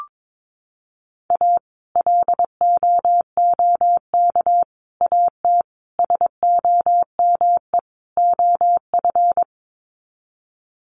Morse code, heard as a keyed tone, about 22 wpm, 700 Hz. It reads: A LOOK AT SOME OF
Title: Sets of 5 Words Encore2 22wpm